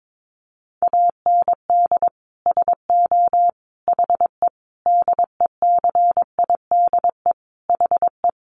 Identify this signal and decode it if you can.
Morse code, heard as a keyed tone, about 22 wpm, 700 Hz. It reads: AND SO HE DECIDE HE
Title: Sentences from Top 500 Words 22wpm